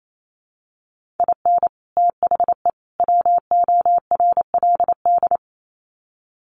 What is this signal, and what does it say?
Morse code, heard as a keyed tone, about 28 wpm, 700 Hz. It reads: IN THE WORLD
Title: Sets of 3 Words 28wpm